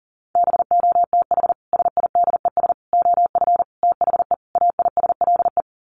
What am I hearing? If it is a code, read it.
Morse code, heard as a keyed tone, about 40 wpm, 700 Hz. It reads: BOTH SIDES OF THE AISLE
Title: Sets of 5 Words Encore2 40wpm